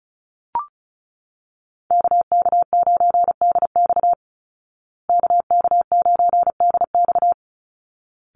Morse code, heard as a keyed tone, about 34 wpm, 700 Hz. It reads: KK9DX KK9DX
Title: Call Signs Repeated 3x - 35wpm